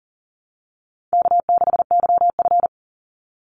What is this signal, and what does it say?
Morse code, heard as a keyed tone, about 40 wpm, 700 Hz. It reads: K6YF
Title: Easy Call Signs Repeated 2x - 40wpm